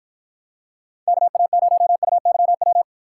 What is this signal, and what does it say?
Morse code, heard as a keyed tone, about 49 wpm, 700 Hz. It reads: KA0UYW